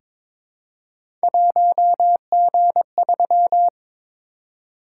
Morse code, heard as a keyed tone, about 22 wpm, 700 Hz. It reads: 1G3
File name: practice / 22wpm / Three Letter-Number Combinations - 22wpm